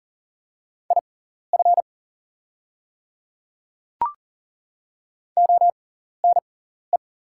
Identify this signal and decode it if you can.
Morse code, heard as a keyed tone, about 40 wpm, spaced as for 14 wpm, 700 Hz. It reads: IF ONE
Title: 6x Character Spacing - Top 500 Words - 40wpm